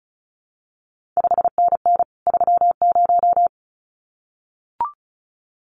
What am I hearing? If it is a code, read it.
Morse code, heard as a keyed tone, about 36 wpm, 700 Hz. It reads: 5NN 30
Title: CQ WW DX Contest Exchange Encore - 35wpm